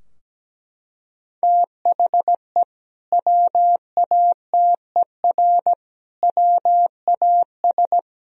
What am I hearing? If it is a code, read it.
Morse code, heard as a keyed tone, about 17 wpm, 700 Hz. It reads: THE WATER WAS
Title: Sentences from Top 1000 Words - Review Part 3 - 17wpm